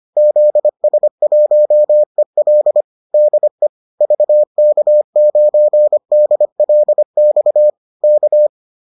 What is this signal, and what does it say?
Morse code, heard as a keyed tone, about 25 wpm, 600 Hz. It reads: ZS1EL DE VK9DLX K
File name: 25wpm-6